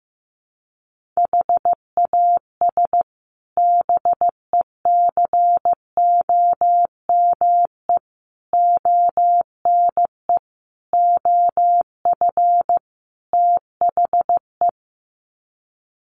Morse code, heard as a keyed tone, about 14 wpm, 700 Hz. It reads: HAS BECOME ONE OF THE
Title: Sets of 5 Words Encore 15wpm